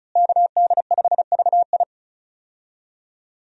Morse code, heard as a keyed tone, about 34 wpm, 700 Hz. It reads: KD5VI
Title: USA Call Signs - 5 Characters - Repeated 3x - 35wpm